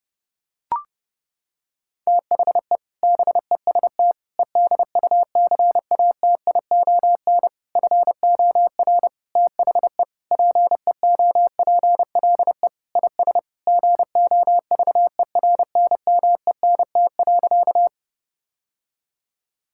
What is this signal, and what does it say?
Morse code, heard as a keyed tone, about 30 wpm, 700 Hz. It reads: THE BEST EDUCATION FOR THE PEOPLE IS GOVERNMENT.
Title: Sentences from Top 1000 Words - Review Part 6 - 30wpm